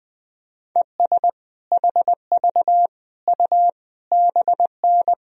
Morse code, heard as a keyed tone, about 19 wpm, 700 Hz. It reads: ES HV U BN
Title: Sentences from Top 200 Words and Abbreviations 20wpm